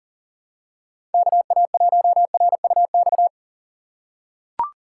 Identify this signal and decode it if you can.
Morse code, heard as a keyed tone, about 38 wpm, 700 Hz. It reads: KA1RUX
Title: USA Call Signs - 6 Characters - 3x Character Spacing - 40wpm